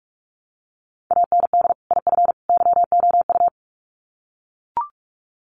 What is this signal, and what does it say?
Morse code, heard as a keyed tone, about 44 wpm, 700 Hz. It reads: AND IF YOU
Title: Sets of 3 Words 45wpm